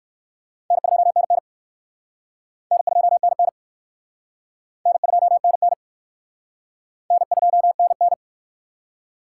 Morse code, heard as a keyed tone, about 66 wpm, 700 Hz. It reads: N2NN N2NN N2NN N2NN